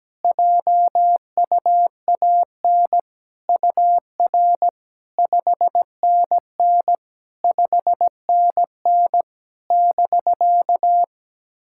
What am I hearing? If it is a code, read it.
Morse code, heard as a keyed tone, about 17 wpm, 700 Hz. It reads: JUAN UR 5NN 5NN <BK>
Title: SOTA QSO Practice 17wpm